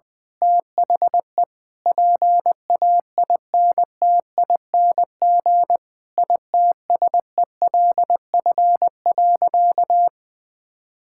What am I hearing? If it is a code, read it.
Morse code, heard as a keyed tone, about 20 wpm, 700 Hz. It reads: THE PAINTING ITSELF.
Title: Sentences from Top 1000 Words - Review Part 6 - 20wpm